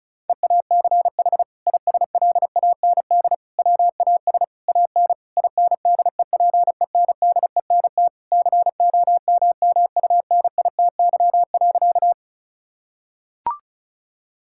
Morse code, heard as a keyed tone, about 36 wpm, 700 Hz. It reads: EACH ISLAND WAS AN INDEPENDENT COMMUNITY.
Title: Sentences from Top 1000 Words - Review Part 6 - 35wpm